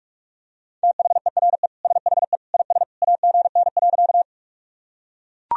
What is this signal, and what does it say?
Morse code, heard as a keyed tone, about 41 wpm, 700 Hz. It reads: THERE SHE IS AGN.
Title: Sentences from Top 200 Words and Abbreviations 45wpm